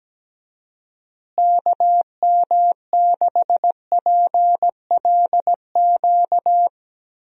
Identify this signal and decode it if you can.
Morse code, heard as a keyed tone, about 17 wpm, 700 Hz. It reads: KM6PLQ
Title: USA Call Signs - 6 Characters - Repeated 3x - 17wpm